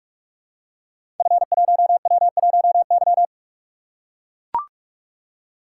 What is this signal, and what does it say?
Morse code, heard as a keyed tone, about 44 wpm, 700 Hz. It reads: F1W1Y
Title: Code Groups - 5 Letter:Numbers - 45wpm